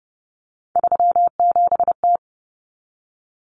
Morse code, heard as a keyed tone, about 30 wpm, 700 Hz. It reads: 37T
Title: Common Cut Numbers - 3-Digit 30wpm